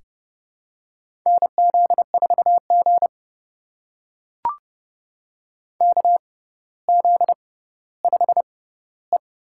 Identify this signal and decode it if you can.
Morse code, heard as a keyed tone, about 30 wpm, spaced as for 15 wpm, 700 Hz. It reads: NZ4G KZ5E
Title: USA Call Signs - 4 Characters - 6x Character Spacing - 30wpm